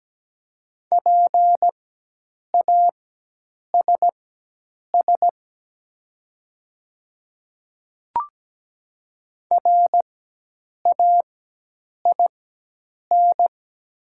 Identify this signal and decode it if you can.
Morse code, heard as a keyed tone, about 17 wpm, spaced as for 8 wpm, 700 Hz. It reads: PASS RAIN
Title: 4x Character Spacing - Top 500 Words - 17wpm